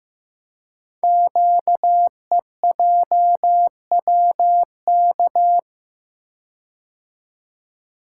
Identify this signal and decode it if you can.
Morse code, heard as a keyed tone, about 15 wpm, 700 Hz. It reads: QEJWK
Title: Code Groups - 5 Letters - 15wpm